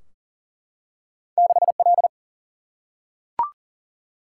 Morse code, heard as a keyed tone, about 40 wpm, 700 Hz. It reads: 6 L
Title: Single-Letter Single-Number 40wpm